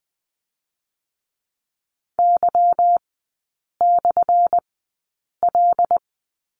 Morse code, heard as a keyed tone, about 20 wpm, spaced as for 8 wpm, 690 Hz. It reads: Y/L
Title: 212 Lesson 34e - Number 8 - Three Characters 20wpm